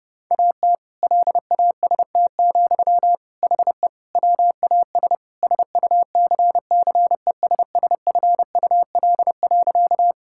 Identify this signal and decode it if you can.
Morse code, heard as a keyed tone, about 30 wpm, 700 Hz. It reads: AT LAST, HE WAS SUCCESSFUL.
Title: Sentences from Top 1000 Words - Review Part 6 - 30wpm